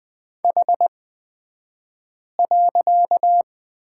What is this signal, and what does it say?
Morse code, heard as a keyed tone, about 20 wpm, spaced as for 5 wpm, 700 Hz. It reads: H.
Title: 254 Lesson 41d - Period - Two Characters 20wpm